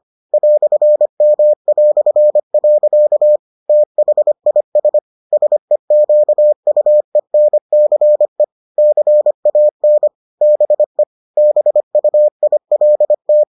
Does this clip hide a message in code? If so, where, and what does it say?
Yes: Morse code from 0.3 to 13.4 s, heard as a keyed tone, about 25 wpm, 600 Hz. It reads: "M". THIS SEQUENCE CAN BE BUILT